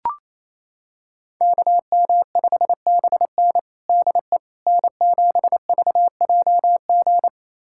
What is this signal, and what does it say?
Morse code, heard as a keyed tone, about 28 wpm, 700 Hz. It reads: KM5BN DE N7VJG
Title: USA Callsign DE Callsign - 5 Characters - 28wpm